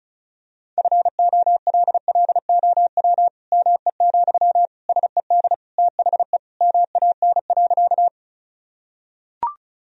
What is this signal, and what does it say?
Morse code, heard as a keyed tone, about 34 wpm, 700 Hz. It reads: FOLLOW ME, SED THE MAN.
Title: Sentences from Top 200 Words and Abbreviations 35wpm